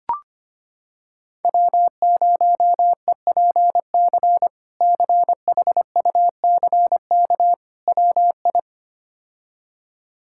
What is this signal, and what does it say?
Morse code, heard as a keyed tone, about 25 wpm, 700 Hz. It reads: W0EPC CHUCK WI
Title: K1USN SST QSO Practice - 25wpm